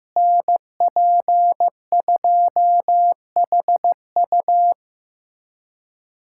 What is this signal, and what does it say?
Morse code, heard as a keyed tone, about 15 wpm, 700 Hz. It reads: NP2HU
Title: USA Callsign DE Callsign - 5 Characters - 15wpm